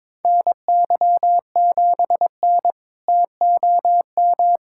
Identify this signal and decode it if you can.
Morse code, heard as a keyed tone, about 22 wpm, 700 Hz. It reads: NY7N TOM
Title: K1USN SST QSO Practice - 22wpm